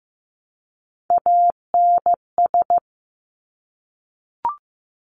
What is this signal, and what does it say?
Morse code, heard as a keyed tone, about 15 wpm, 700 Hz. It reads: ANS
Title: Top Three Letter Combinations in Words 15wpm